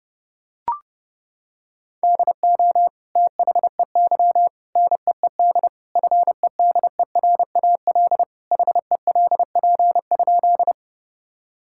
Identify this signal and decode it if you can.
Morse code, heard as a keyed tone, about 30 wpm, 700 Hz. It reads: DO THEY NEED FEDERAL HELP?
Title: Sentences from Top 1000 Words - Review Part 5 - 30wpm